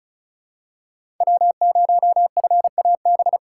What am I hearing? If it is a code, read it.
Morse code, heard as a keyed tone, about 36 wpm, 700 Hz. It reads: W0FAB
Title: USA Call Signs - 5 Characters - 4x Character Spacing - 35wpm